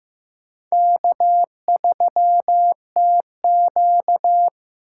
Morse code, heard as a keyed tone, about 15 wpm, 700 Hz. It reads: K3TQ
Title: USA Call Signs - 4 Characters - 6x Character Spacing - 15wpm